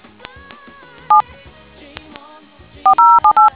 DTMF, heard as telephone touch tones, tones of about 120 ms. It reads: 7 4078